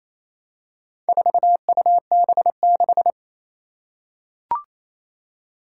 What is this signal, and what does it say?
Morse code, heard as a keyed tone, about 27 wpm, 700 Hz. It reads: 4UB6
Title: Four Challenging Letter-Number Combinations 28wpm